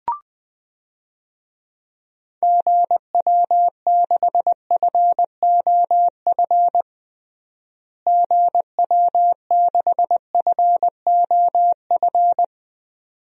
Morse code, heard as a keyed tone, about 20 wpm, 690 Hz. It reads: GW6FOF GW6FOF
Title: DXCC-100 Call Signs - 6 Characters - Repeated 2x - 20wpm